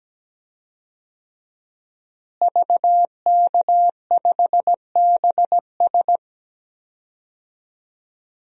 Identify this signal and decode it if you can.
Morse code, heard as a keyed tone, about 17 wpm, 700 Hz. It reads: VK5BS
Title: DXCC-100 Call Signs - 5 Characters - Repeated 2x - 17wpm